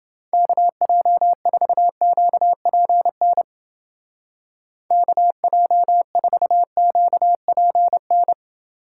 Morse code, heard as a keyed tone, about 29 wpm, 700 Hz. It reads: KJ4QPN KJ4QPN